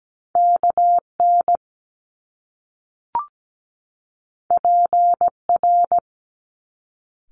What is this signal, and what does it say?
Morse code, heard as a keyed tone, about 17 wpm, 700 Hz. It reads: KN PR
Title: Top Two Letter Combinations in Words 17wpm